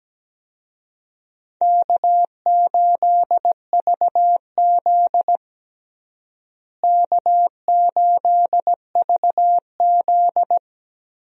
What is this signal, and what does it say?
Morse code, heard as a keyed tone, about 17 wpm, 700 Hz. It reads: K8VZ K8VZ
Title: USA Call Signs - 4 Characters - Repeated 3x - 17wpm